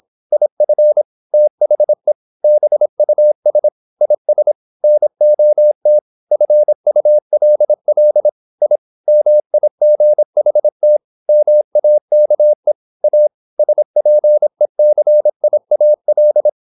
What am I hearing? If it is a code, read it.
Morse code, heard as a keyed tone, about 26 wpm, 600 Hz. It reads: IF THE BUS IS NOT FULL I MIGHT MAKE A SPECIAL
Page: Never 26wpm | CW med Gnister